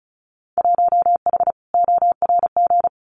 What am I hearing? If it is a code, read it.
Morse code, heard as a keyed tone, about 34 wpm, 700 Hz. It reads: 1H ORG
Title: Winter Field Day Exchanges 35wpm